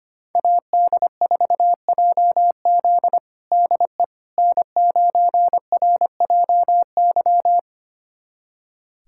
Morse code, heard as a keyed tone, about 25 wpm, 700 Hz. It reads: AD4JZ DE N9RJY
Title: USA Callsign DE Callsign - 5 Characters - 25wpm